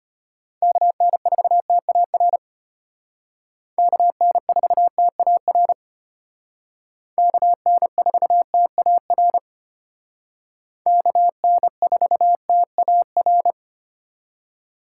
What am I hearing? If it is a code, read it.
Morse code, heard as a keyed tone, about 32 wpm, 700 Hz. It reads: KN4TAR KN4TAR KN4TAR KN4TAR